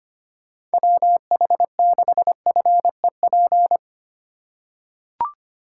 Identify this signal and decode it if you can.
Morse code, heard as a keyed tone, about 25 wpm, 700 Hz. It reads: WH6FEP
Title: USA Call Signs - 4x Character Spacing - 25wpm